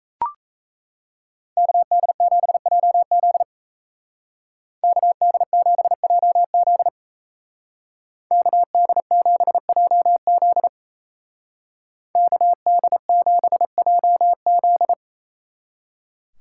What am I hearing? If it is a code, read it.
Morse code, heard as a keyed tone, about 37 wpm, 700 Hz. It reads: KD7JZ KD7JZ KD7JZ KD7JZ